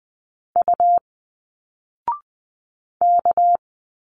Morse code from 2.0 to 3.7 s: K